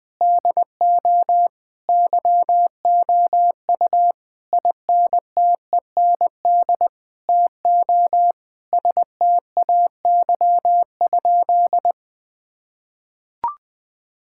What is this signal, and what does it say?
Morse code, heard as a keyed tone, about 21 wpm, 700 Hz. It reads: DO YOU INTEND TO STAY?
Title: Sentences from Top 1600 Words 20wpm